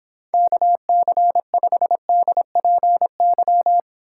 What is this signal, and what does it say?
Morse code, heard as a keyed tone, about 26 wpm, 700 Hz. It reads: KC5DPY